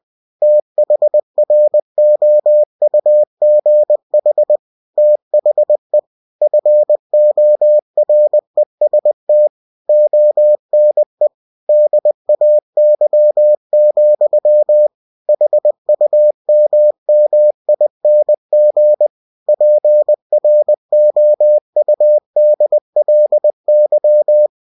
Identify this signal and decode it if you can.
Morse code, heard as a keyed tone, about 20 wpm, 600 Hz. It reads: THROUGH THE FOREST ONE DAY, HUMMING PROUDLY